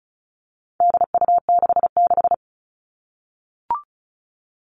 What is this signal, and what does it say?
Morse code, heard as a keyed tone, about 34 wpm, 700 Hz. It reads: DU66
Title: All Cut Numbers - 4-Digit 35wpm